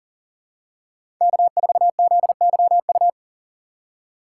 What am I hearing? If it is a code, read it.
Morse code, heard as a keyed tone, about 37 wpm, 700 Hz. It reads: K4ZYU
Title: USA Call Signs - 5 Characters - 3x Character Spacing - 40wpm